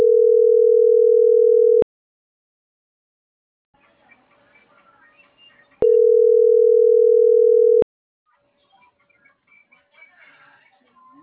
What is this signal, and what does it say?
Ringback tone (North America), 440 + 480 Hz.